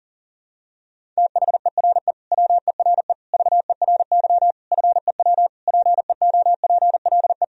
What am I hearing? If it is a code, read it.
Morse code, heard as a keyed tone, about 40 wpm, 700 Hz. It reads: THERE WERE VERY FEW PEOPLE
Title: Sentences from Top 1000 Words - Review Part 2 - 40wpm